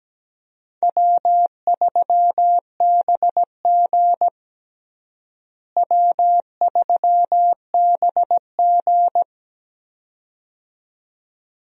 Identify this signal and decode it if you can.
Morse code, heard as a keyed tone, about 17 wpm, 700 Hz. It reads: W3BG W3BG
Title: USA Call Signs - 4 Characters - Repeated 2x - 17wpm